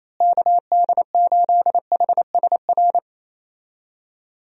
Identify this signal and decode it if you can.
Morse code, heard as a keyed tone, about 27 wpm, 700 Hz. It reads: KD8HSR
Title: USA Call Signs - 6 Characters - Repeated 2x - 28wpm